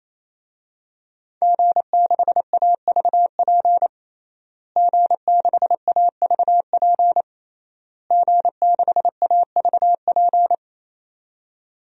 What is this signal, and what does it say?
Morse code, heard as a keyed tone, about 28 wpm, 700 Hz. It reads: G6AVP G6AVP G6AVP
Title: DXCC-100 Call Signs - 5 Characters - Repeated 3x - 28wpm